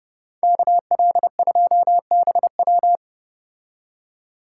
Morse code, heard as a keyed tone, about 29 wpm, 700 Hz. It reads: KL2BW